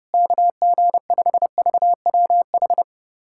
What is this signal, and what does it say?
Morse code, heard as a keyed tone, about 32 wpm, 700 Hz. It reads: KG5VWH
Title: USA Callsign DE Callsign 30wpm